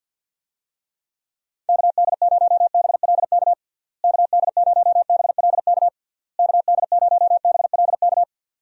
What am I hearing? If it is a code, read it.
Morse code, heard as a keyed tone, about 52 wpm, 700 Hz. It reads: KD0BLK KD0BLK KD0BLK
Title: USA Call Signs - 6 Characters - Repeated 3x - 50wpm